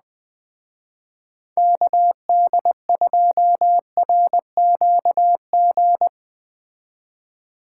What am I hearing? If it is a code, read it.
Morse code, heard as a keyed tone, about 20 wpm, 700 Hz. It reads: KD2RQG
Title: Speed Racing - USA 2x3 Call Signs 17wpm